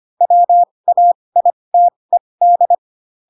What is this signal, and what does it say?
Morse code, heard as a keyed tone, about 25 wpm, spaced as for 20 wpm, 700 Hz. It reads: WAITED
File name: Pinocchio / Pinocchio ch1 15 25 ik1hge